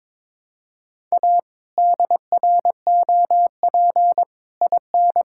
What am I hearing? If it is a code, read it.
Morse code, heard as a keyed tone, about 22 wpm, 700 Hz. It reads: A DROP IN
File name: practice / 22wpm / Sets of 7 Words - 22wpm